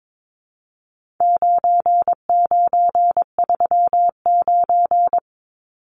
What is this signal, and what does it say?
Morse code, heard as a keyed tone, about 22 wpm, 700 Hz. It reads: 9939
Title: Numbers - 4-Digit 22wpm